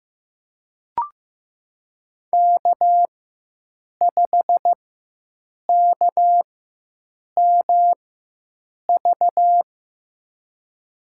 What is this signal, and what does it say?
Morse code, heard as a keyed tone, about 15 wpm, spaced as for 7 wpm, 700 Hz. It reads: K5KMV
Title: USA Call Signs - 5 Characters - 4x Character Spacing - 15wpm